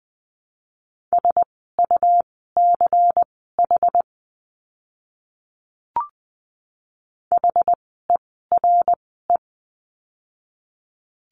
Morse code, heard as a keyed tone, about 20 wpm, spaced as for 15 wpm, 700 Hz. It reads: SUCH HERE
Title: 230 Lesson 37f - Letter x - Words 20wpm